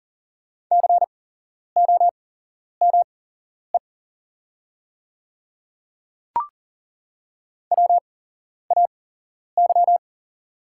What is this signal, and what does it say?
Morse code, heard as a keyed tone, about 40 wpm, spaced as for 11 wpm, 700 Hz. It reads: COME WAY
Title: 8x Character Spacing - Top 200 Words - 40wpm